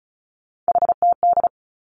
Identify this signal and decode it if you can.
Morse code, heard as a keyed tone, about 34 wpm, 700 Hz. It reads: HTD
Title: Three Challenging Letter-Number Combinations 35wpm